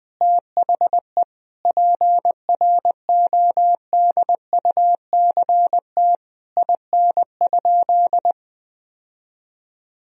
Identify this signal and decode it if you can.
Morse code, heard as a keyed tone, about 21 wpm, 700 Hz. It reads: THE PRODUCT IN?
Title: Sentences from Top 1000 Words - Review Part 2 - 20wpm